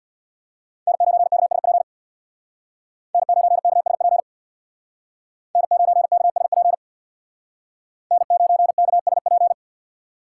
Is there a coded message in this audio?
Yes: Morse code from 0.9 to 9.5 s, heard as a keyed tone, about 63 wpm, 700 Hz. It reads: N9KSP N9KSP N9KSP N9KSP